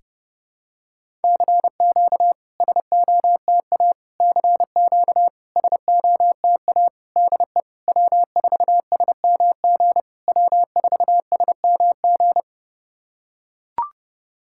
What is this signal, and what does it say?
Morse code, heard as a keyed tone, about 30 wpm, 700 Hz. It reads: CQ SOTA CQ SOTA DE W4SMG W4SMG
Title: SOTA QSO Practice 30wpm